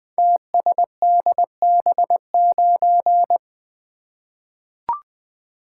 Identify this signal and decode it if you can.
Morse code, heard as a keyed tone, about 21 wpm, 700 Hz. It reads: TSDB9
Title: Code Groups - 5 Letter:Numbers - 20wpm